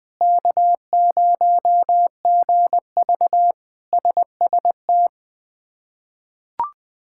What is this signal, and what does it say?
Morse code, heard as a keyed tone, about 19 wpm, 700 Hz. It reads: K0GV SST
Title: K1USN SST QSO Practice - 20wpm